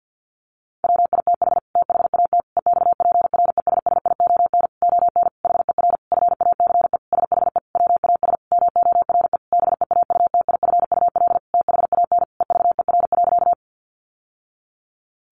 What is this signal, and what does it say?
Morse code, heard as a keyed tone, about 49 wpm, 700 Hz. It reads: WITH THAT EXPRESSION ON HER FACE SHE WAS MORE BEAUTIFUL THAN EVER.
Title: Sentences from Top 1200 Words 50wpm